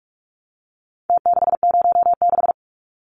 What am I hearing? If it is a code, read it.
Morse code, heard as a keyed tone, about 44 wpm, 700 Hz. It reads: T606
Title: Numbers - Contest Style - 4-Digit 45wpm